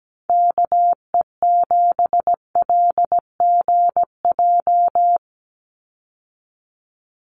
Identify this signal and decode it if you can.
Morse code, heard as a keyed tone, about 17 wpm, 700 Hz. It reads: KE7LGJ
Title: USA Call Signs 17wpm